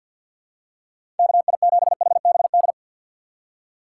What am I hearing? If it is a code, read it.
Morse code, heard as a keyed tone, about 53 wpm, 700 Hz. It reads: KI7HBD
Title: USA Call Signs - 6 Characters - 3x Character Spacing - 50wpm